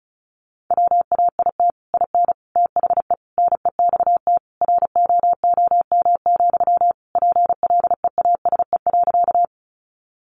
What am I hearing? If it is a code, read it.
Morse code, heard as a keyed tone, about 34 wpm, 700 Hz. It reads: WAIT IN THE NEXT ROOM, PLEASE.
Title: Sentences from Top 1500 Words 35wpm